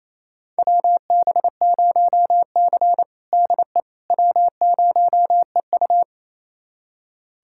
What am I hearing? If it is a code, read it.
Morse code, heard as a keyed tone, about 28 wpm, 700 Hz. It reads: WB0C DE W0EU
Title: USA Callsign DE Callsign - 4 Characters - 28wpm